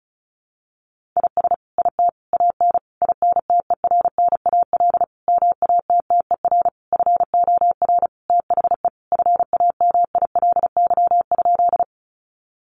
Morse code, heard as a keyed tone, about 34 wpm, 700 Hz. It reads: IS IT AN INTERNAL MATTER FOR THE FAMILY?
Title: Sentences from Top 1400 Words 35wpm